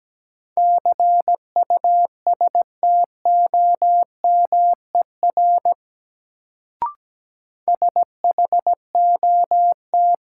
Morse code, heard as a keyed tone, about 17 wpm, 700 Hz. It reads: CUSTOMER SHOT
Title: Sentences from Top 1100 Words - New Word Drill - 17wpm